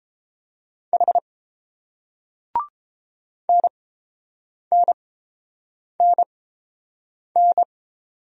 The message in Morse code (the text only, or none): H NNNN